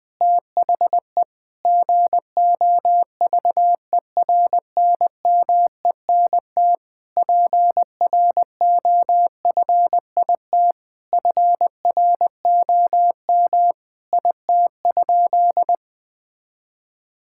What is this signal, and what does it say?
Morse code, heard as a keyed tone, about 20 wpm, 700 Hz. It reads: THE GOVERNMENT PROFIT FROM IT?
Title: Sentences from Top 1000 Words - Review Part 5 - 20wpm